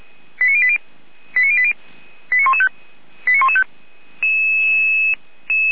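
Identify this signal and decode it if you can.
Five-tone selective call (ZVEI), heard as quick runs of five tones, tones of about 70 ms. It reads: 89080 89080 89115 89115
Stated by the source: ZVEI five-tone selective call received on an SDR by Feuerwehr Orlamünde.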